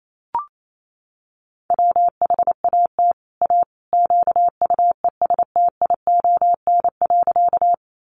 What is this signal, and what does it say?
Morse code, heard as a keyed tone, about 28 wpm, 700 Hz. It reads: WHAT A QUESTION.
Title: Sentences from Top 400 Words 28wpm